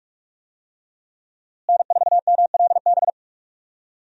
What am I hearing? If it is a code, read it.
Morse code, heard as a keyed tone, about 46 wpm, 700 Hz. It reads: N4MLB
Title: USA Call Signs - 5 Characters - Repeated 3x - 45wpm